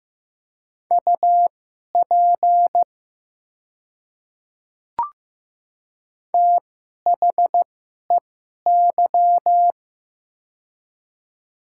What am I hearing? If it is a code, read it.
Morse code, heard as a keyed tone, about 15 wpm, spaced as for 11 wpm, 700 Hz. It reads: UP THEY
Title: 2x Character Spacing - Top 100 Words - 15wpm